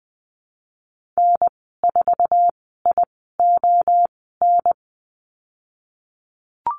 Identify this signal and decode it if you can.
Morse code, heard as a keyed tone, about 20 wpm, spaced as for 15 wpm, 700 Hz. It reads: N4ION
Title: 048 Lesson 9g - Number 4 - Call Signs 20wpm